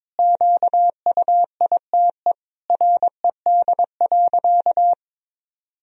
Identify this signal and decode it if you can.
Morse code, heard as a keyed tone, about 22 wpm, 700 Hz. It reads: QUITE RED.
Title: Sentences from Top 1000 Words - Review Part 2 - 22wpm